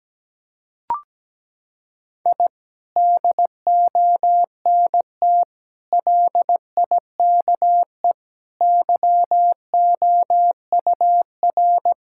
Morse code, heard as a keyed tone, about 17 wpm, 700 Hz. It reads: I DONT LIKE YOUR
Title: Sentences from Top 1000 Words - Review Part 6 - 17wpm